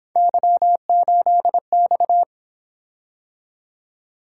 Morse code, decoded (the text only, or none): Y8X